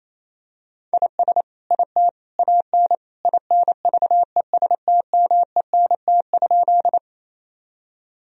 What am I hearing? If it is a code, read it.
Morse code, heard as a keyed tone, about 27 wpm, 700 Hz. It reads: IS IT AN INVESTMENT?
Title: Sentences from Top 1000 Words - Review Part 4 - 28wpm